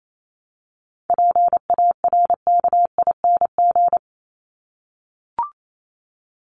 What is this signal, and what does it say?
Morse code, heard as a keyed tone, about 28 wpm, 700 Hz. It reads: PARKING
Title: Top 1501-1750 Words 28wpm